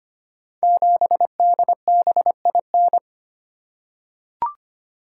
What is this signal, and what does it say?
Morse code, heard as a keyed tone, about 25 wpm, 690 Hz. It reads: 7DBIN
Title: Code Groups - 5 Letter:Numbers - 25wpm